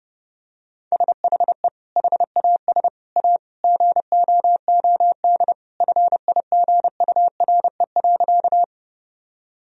Morse code, heard as a keyed tone, about 30 wpm, 700 Hz. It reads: SHE HAS A GOOD FIGURE.
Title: Sentences from Top 1000 Words - Review Part 2 - 30wpm